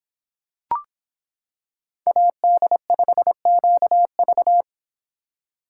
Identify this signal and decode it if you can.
Morse code, heard as a keyed tone, about 26 wpm, 700 Hz. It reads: AD5QV